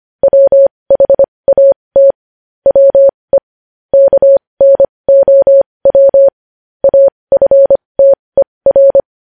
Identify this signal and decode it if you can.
Morse code, heard as a keyed tone, about 25 wpm, spaced as for 20 wpm, 550 Hz. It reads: WHAT WE KNOW AFTER
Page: news-headlines-in-morse-code-at-20-wpm